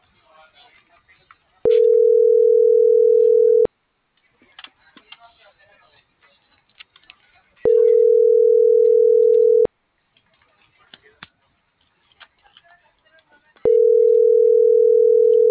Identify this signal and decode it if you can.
Ringback tone (North America), 440 + 480 Hz.